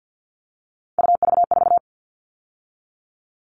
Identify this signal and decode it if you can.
Morse code, heard as a keyed tone, about 50 wpm, 700 Hz. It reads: UV4
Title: All Cut Numbers - 3-Digit 50wpm